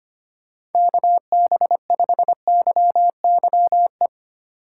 Morse code, heard as a keyed tone, about 33 wpm, 700 Hz. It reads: KB5YYE